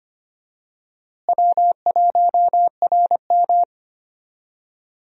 Morse code, heard as a keyed tone, about 25 wpm, 700 Hz. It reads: W1RM
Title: USA Call Signs - 4 Characters - Repeated 3x - 25wpm